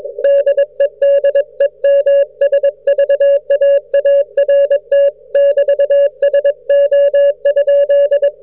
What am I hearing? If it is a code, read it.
Morse code, heard as a keyed tone, about 21 wpm, 560 Hz. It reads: DEDEMSVAART <BT> SO?